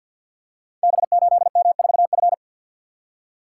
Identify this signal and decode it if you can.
Morse code, heard as a keyed tone, about 52 wpm, 700 Hz. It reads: B8M4F
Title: Code Groups - 5 Letter:Numbers - 50wpm